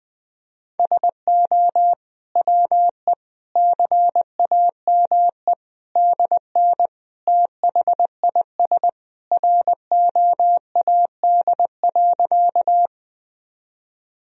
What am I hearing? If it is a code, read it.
Morse code, heard as a keyed tone, about 19 wpm, 700 Hz. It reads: SO WE CAME DN THIS ROAD.
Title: Sentences from Top 400 Words and Abbreviations 20wpm